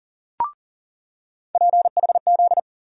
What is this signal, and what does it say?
Morse code, heard as a keyed tone, about 40 wpm, 700 Hz. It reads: PHZ
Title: USA Call Sign Postfixes - 3 Letters 40wpm